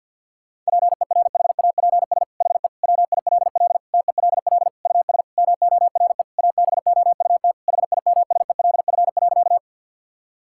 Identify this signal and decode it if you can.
Morse code, heard as a keyed tone, about 50 wpm, 700 Hz. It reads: PERHAPS HE WILL TELL US MORE ABOUT HIMSELF.
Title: Sentences from Top 1600 Words 50wpm